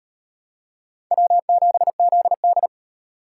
Morse code, heard as a keyed tone, about 38 wpm, 700 Hz. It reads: W7ZD